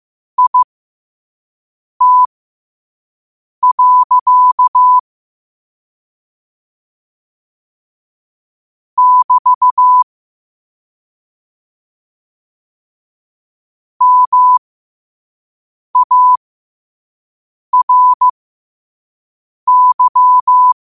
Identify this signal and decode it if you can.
Morse code, heard as a keyed tone, about 15 wpm, spaced as for 5 wpm, 1000 Hz. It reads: IT. <BT> MARY
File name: qotdfiles / 5wpm